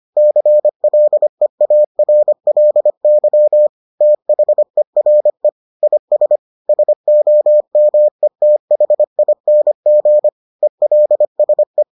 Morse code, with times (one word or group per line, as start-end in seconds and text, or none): 0.2-3.7 CLEARLY
4.0-5.5 THERE
5.8-6.4 IS
6.7-10.3 SOMETHING
10.6-11.8 ELSE